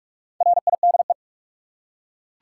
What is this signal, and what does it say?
Morse code, heard as a keyed tone, about 44 wpm, 700 Hz. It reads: RIDE
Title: Four Letter Words - 45wpm